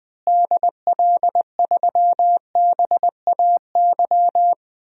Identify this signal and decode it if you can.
Morse code, heard as a keyed tone, about 21 wpm, 700 Hz. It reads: DL3BAY
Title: DXCC-100 Call Signs - 6 Characters - Repeated 3x - 20wpm